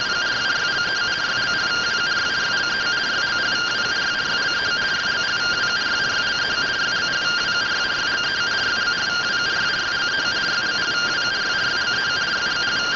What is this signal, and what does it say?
Two-tone teleprinter signal, about 170 Hz shift.